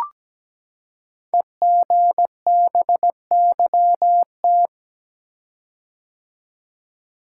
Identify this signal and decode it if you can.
Morse code, heard as a keyed tone, about 17 wpm, 700 Hz. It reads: EGBYT
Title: Code Groups - 5 Letter:Numbers - 17wpm